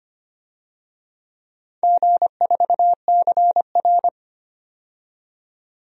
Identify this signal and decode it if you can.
Morse code, heard as a keyed tone, about 25 wpm, 700 Hz. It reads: G4CR
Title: DXCC-100 Call Signs - 4 Characters - 25wpm